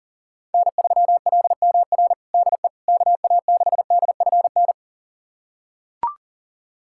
Morse code, heard as a keyed tone, about 40 wpm, 700 Hz. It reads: N3LMR DE KA6DFN
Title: USA Callsign DE Callsign 40wpm